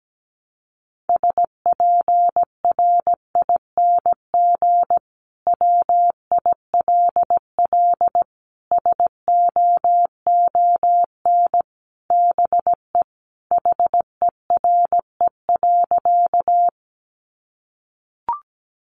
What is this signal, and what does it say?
Morse code, heard as a keyed tone, about 17 wpm, 700 Hz. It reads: SPRING WILL SOON BE HERE.
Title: Sentences from Top 1200 Words 17wpm